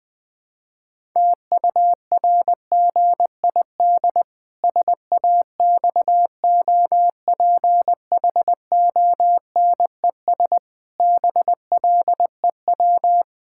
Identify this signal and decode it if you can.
Morse code, heard as a keyed tone, about 21 wpm, 700 Hz. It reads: TURGID SAXOPHONES BLEW
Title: Pangram Challenge v2 - 20wpm